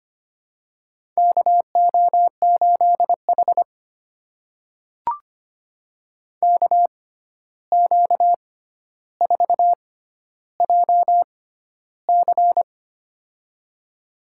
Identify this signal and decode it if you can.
Morse code, heard as a keyed tone, about 25 wpm, spaced as for 12 wpm, 700 Hz. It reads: KO8H KQ4JC
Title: USA Call Signs - 6x Character Spacing - 25wpm